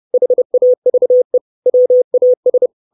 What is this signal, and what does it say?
Morse code, heard as a keyed tone, about 30 wpm, 500 Hz. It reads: HAVE WAS
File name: 632 Words 30WPM